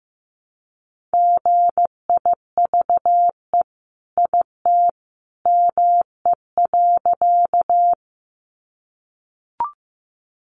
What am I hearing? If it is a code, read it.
Morse code, heard as a keyed tone, about 15 wpm, 700 Hz. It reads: GIVE IT ME.
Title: Sentences from Top 200 Words 15wpm